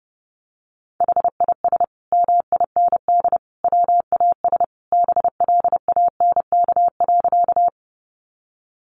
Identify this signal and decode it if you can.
Morse code, heard as a keyed tone, about 30 wpm, 700 Hz. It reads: HIS MIND WAS BLANK.
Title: Sentences from Top 1500 Words 30wpm